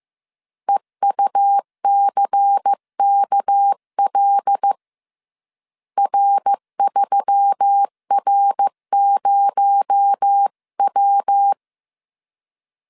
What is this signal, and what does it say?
Morse code, heard as a keyed tone, about 15 wpm, 800 Hz. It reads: EUCKL R3R0W
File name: CW / DET / 1 6 1500C S B 14wpm 2958